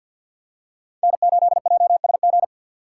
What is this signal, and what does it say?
Morse code, heard as a keyed tone, about 49 wpm, 700 Hz. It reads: N8JSG
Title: USA Call Signs - 5 Characters - 3x Character Spacing - 50wpm